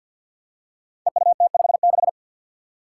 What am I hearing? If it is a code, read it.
Morse code, heard as a keyed tone, about 48 wpm, 700 Hz. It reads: EUT56
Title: Code Groups - 5 Letter:Numbers - 50wpm